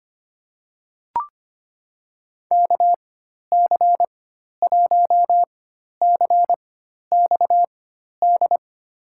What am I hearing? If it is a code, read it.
Morse code, heard as a keyed tone, about 25 wpm, spaced as for 12 wpm, 700 Hz. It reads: KC1CXD
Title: USA Call Signs - 6 Characters - 4x Character Spacing - 25wpm